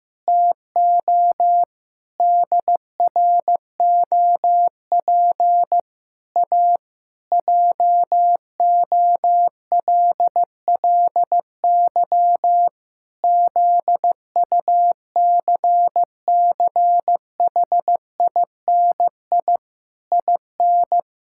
Morse code, heard as a keyed tone, about 15 wpm, 700 Hz. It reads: TO DROP A JOLLY ZUCCHINI IN
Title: Pangram Challenge v2 - 15wpm